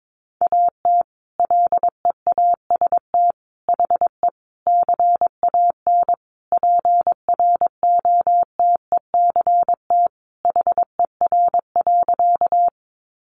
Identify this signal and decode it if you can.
Morse code, heard as a keyed tone, about 22 wpm, 700 Hz. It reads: AT LEAST HE CAN PROTECT HER.
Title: Sentences from Top 900 Words 22wpm